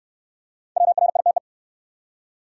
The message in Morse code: WRITE

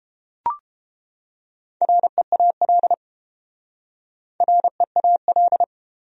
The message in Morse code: REAL REAL

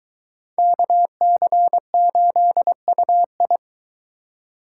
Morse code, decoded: KC8UI

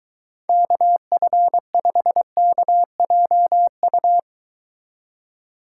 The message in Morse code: KF5KJU

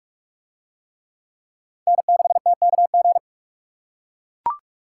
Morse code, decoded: N6TKG

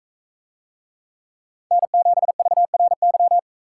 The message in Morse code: N7VRY

